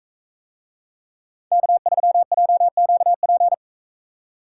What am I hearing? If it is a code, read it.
Morse code, heard as a keyed tone, about 42 wpm, 700 Hz. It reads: K3JQP